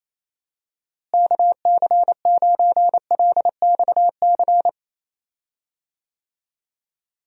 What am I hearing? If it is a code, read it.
Morse code, heard as a keyed tone, about 27 wpm, 700 Hz. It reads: KC9LXC
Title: USA Call Signs - 6 Characters - Repeated 2x - 28wpm